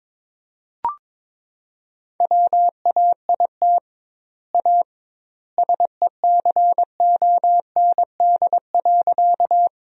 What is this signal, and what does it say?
Morse code, heard as a keyed tone, about 22 wpm, 700 Hz. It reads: WAIT A SECOND.
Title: 2x Word-Spacing - Sentences from Top 500 Words - 22wpm